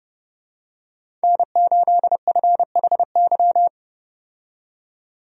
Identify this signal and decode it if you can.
Morse code, heard as a keyed tone, about 31 wpm, 700 Hz. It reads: N8FHY
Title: USA Call Signs - 5 Characters - 30wpm